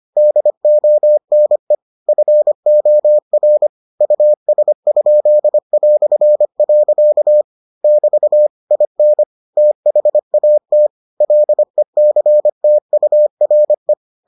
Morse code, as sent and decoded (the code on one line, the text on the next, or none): -.. --- -. / ..-. --- .-. / ..- ... ..--.. .-..-. .-.-.- / -...- / .. -. / - .... .- - / .-.. -.-. - ..- .-. .
DONE FOR US?". <BT> IN THAT LECTURE